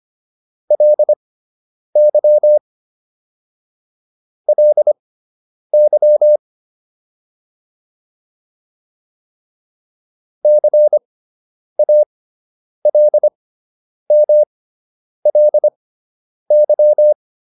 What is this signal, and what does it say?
Morse code, heard as a keyed tone, about 25 wpm, spaced as for 9 wpm, 600 Hz. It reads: LY LY CALMLY